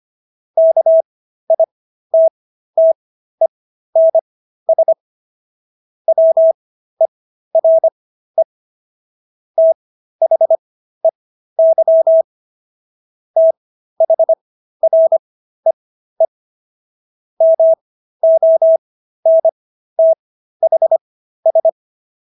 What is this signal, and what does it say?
Morse code, heard as a keyed tone, about 25 wpm, spaced as for 13 wpm, 650 Hz. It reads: KITTENS WERE THEY THREE MONTHS